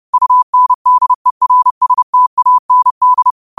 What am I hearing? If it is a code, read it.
Morse code, heard as a keyed tone, about 30 wpm, 1000 Hz. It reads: UNDERSTAND
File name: qotdfiles / 30wpm